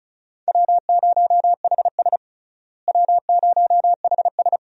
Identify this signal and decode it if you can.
Morse code, heard as a keyed tone, about 36 wpm, 700 Hz. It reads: W0HS W0HS
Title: USA Call Signs - 4 Characters - Repeated 2x - 35wpm